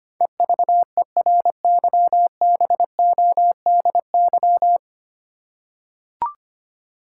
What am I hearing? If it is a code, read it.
Morse code, heard as a keyed tone, about 25 wpm, 700 Hz. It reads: EVERYBODY
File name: Nine Letter Words - 25wpm